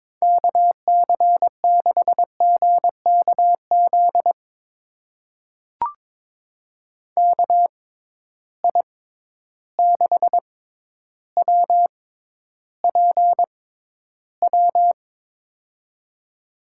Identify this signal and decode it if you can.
Morse code, heard as a keyed tone, about 22 wpm, spaced as for 11 wpm, 700 Hz. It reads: KC6GKZ KI6WPW
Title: USA Call Signs - 6 Characters - 6x Character Spacing - 22wpm